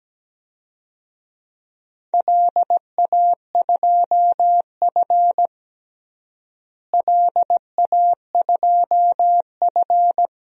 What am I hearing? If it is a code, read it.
Morse code, heard as a keyed tone, about 17 wpm, 700 Hz. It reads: LA2F LA2F
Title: DXCC-100 Call Signs - 4 Characters - Repeated 2x - 17wpm